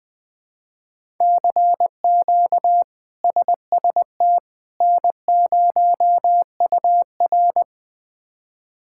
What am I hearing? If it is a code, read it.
Morse code, heard as a keyed tone, about 21 wpm, 700 Hz. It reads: CQ SST N0UR
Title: K1USN SST QSO Practice - 20wpm